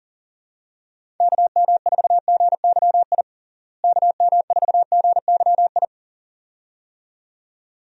Morse code, heard as a keyed tone, about 37 wpm, 700 Hz. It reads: KM4GYI KM4GYI
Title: USA Call Signs - 6 Characters - Repeated 2x - 40wpm